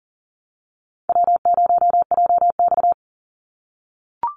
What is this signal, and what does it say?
Morse code, heard as a keyed tone, about 40 wpm, 700 Hz. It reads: W0JX
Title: Easy Call Signs - 40wpm